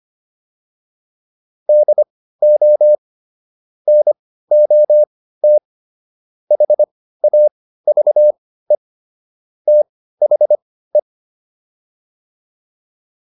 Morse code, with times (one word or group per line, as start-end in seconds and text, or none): 1.7-3.0 DO
3.9-5.6 NOT
6.5-8.8 HAVE
9.7-11.0 THE